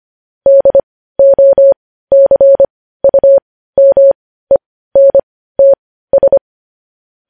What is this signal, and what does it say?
Morse code, heard as a keyed tone, about 25 wpm, spaced as for 15 wpm, 550 Hz. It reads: DOCUMENTS
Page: news-headlines-in-morse-code-at-15-wpm